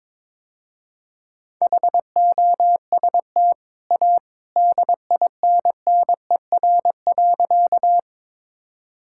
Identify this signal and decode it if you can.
Morse code, heard as a keyed tone, about 22 wpm, 690 Hz. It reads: HOST A DINNER.
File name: practice / 22wpm / Easy Sentences with Hints - 22wpm